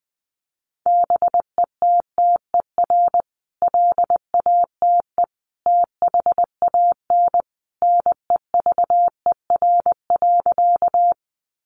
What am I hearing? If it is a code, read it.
Morse code, heard as a keyed tone, about 20 wpm, 700 Hz. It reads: BETTER LATE THAN NEVER.
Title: Sentences from Top 400 Words 20wpm